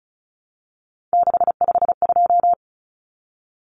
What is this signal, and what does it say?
Morse code, heard as a keyed tone, about 34 wpm, 700 Hz. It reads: 652
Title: Numbers - 3-Digit 35wpm